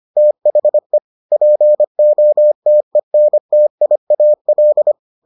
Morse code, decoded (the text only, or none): THE POTENTIAL